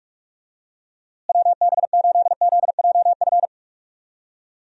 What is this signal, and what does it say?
Morse code, heard as a keyed tone, about 46 wpm, 700 Hz. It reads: WB8ZJF
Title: USA Call Signs - 6 Characters - Repeated 3x - 45wpm